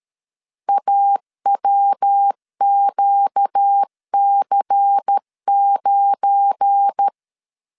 Morse code, heard as a keyed tone, about 13 wpm, 800 Hz. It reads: AWQC9
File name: CW / DET / 1 9 1500C S A 12wpm 3800